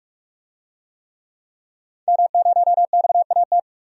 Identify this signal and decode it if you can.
Morse code, heard as a keyed tone, about 44 wpm, 700 Hz. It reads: M0XAT
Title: DXCC-100 Call Signs - 5 Characters - 45wpm